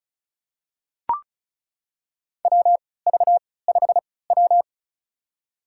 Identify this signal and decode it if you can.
Morse code, heard as a keyed tone, about 34 wpm, spaced as for 20 wpm, 700 Hz. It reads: WV5W
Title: USA Call Signs - 4 Characters - 3x Character Spacing - 35wpm